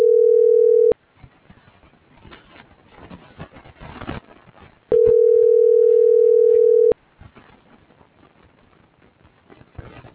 Ringback tone (North America), 440 + 480 Hz.